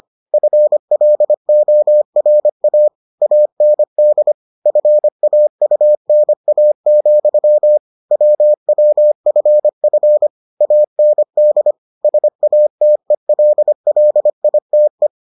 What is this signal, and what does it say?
Morse code, heard as a keyed tone, about 25 wpm, 600 Hz. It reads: FLORA AND FAUNA, WWFF AND SATELLITE